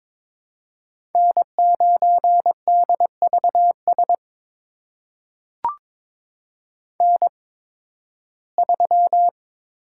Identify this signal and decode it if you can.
Morse code, heard as a keyed tone, about 22 wpm, spaced as for 14 wpm, 700 Hz. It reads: N9DVS N3
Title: USA Call Signs - 5 Characters - 8x Character Spacing - 22wpm